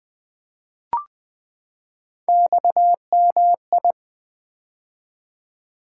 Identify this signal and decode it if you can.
Morse code, heard as a keyed tone, about 21 wpm, 700 Hz. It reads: XMI
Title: Three Letter-Number Combinations - 20wpm